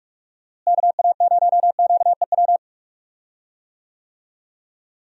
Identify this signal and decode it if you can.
Morse code, heard as a keyed tone, about 46 wpm, 700 Hz. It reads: KA0QEW
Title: USA Call Signs - 6 Characters - Repeated 2x - 45wpm